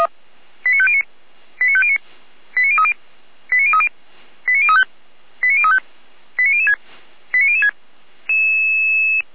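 Five-tone selective call (ZVEI), heard as quick runs of five tones, tones of about 80 ms. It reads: 89509 89509 89030 89030 89035 89035 89006 89006
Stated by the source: ZVEI five-tone selective call received on SDR by Feuerwehr Orlamünde fire brigade.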